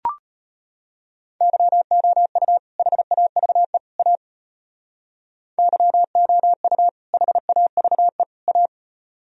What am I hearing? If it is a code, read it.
Morse code, heard as a keyed tone, about 32 wpm, 700 Hz. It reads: YOU HAVE A YOU HAVE A